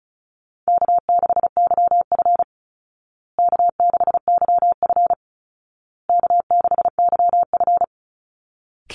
Morse code, heard as a keyed tone, about 34 wpm, 700 Hz. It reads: K6YF K6YF K6YF
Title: Easy Call Signs Repeated 3x - 35wpm